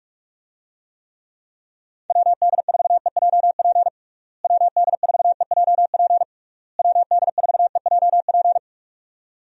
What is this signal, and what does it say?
Morse code, heard as a keyed tone, about 46 wpm, 700 Hz. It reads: WD4EJP WD4EJP WD4EJP
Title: USA Call Signs - 6 Characters - Repeated 3x - 45wpm